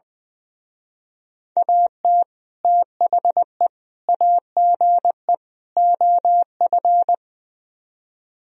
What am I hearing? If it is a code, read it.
Morse code, heard as a keyed tone, about 20 wpm, 700 Hz. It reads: AT THE AGE OF
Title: Speed-Racing - Sets of 4 Words 20wpm